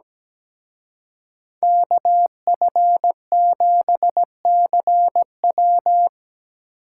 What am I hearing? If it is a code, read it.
Morse code, heard as a keyed tone, about 19 wpm, 700 Hz. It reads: KF7CW
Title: Speed Racing - USA 2x2 Call Signs 17wpm